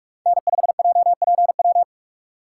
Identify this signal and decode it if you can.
Morse code, heard as a keyed tone, about 41 wpm, 700 Hz. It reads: N5JPW
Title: USA Call Signs - 5 Characters - 3x Character Spacing - 45wpm